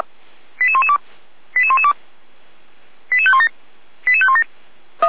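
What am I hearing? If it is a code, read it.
Five-tone selective call (ZVEI), heard as quick runs of five tones, tones of about 70 ms. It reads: 88112 88112 88517 88517
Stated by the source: ZVEI five-tone call received on an SDR.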